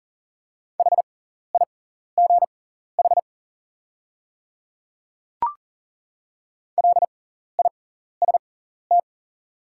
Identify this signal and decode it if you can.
Morse code, heard as a keyed tone, about 40 wpm, spaced as for 14 wpm, 700 Hz. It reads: HIGH LIST
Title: 6x Character Spacing - Top 500 Words - 40wpm